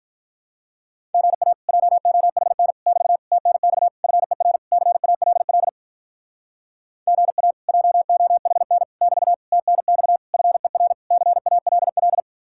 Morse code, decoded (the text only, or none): GA JOHN <BT> TNX FER CALL GA JOHN <BT> TNX FER CALL